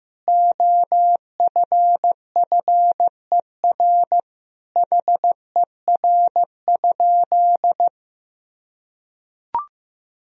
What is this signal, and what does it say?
Morse code, heard as a keyed tone, about 15 wpm, 700 Hz. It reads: OFFER HER?
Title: Sentences from Top 1000 Words - Review Part 4 - 15wpm